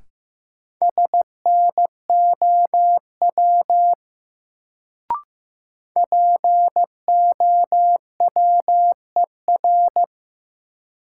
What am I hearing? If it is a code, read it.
Morse code, heard as a keyed tone, about 15 wpm, 700 Hz. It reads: SNOW POWER
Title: Sentences from Top 500 Words - New Word Drill - 15wpm